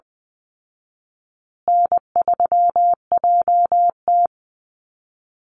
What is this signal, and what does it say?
Morse code, heard as a keyed tone, about 20 wpm, 700 Hz. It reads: N3JT
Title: Call Signs Repeated 2x - 20wpm